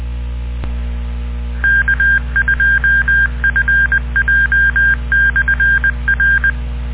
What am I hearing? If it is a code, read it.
Morse code, heard as a keyed tone, about 20 wpm, 1600 Hz. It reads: K2FJ/R